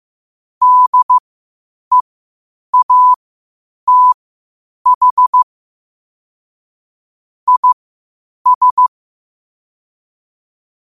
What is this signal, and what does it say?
Morse code, heard as a keyed tone, about 15 wpm, spaced as for 8 wpm, 1000 Hz. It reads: DEATH IS
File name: qotdfiles / 8wpm